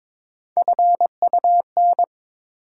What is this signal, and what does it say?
Morse code, heard as a keyed tone, about 21 wpm, 700 Hz. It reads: FUN
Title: Sentences from Top 700 Words - New Word Drill - 22wpm